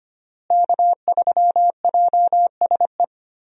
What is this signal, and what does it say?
Morse code, heard as a keyed tone, about 25 wpm, 700 Hz. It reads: K3JSE
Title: USA Call Signs - 5 Characters - Repeated 3x - 25wpm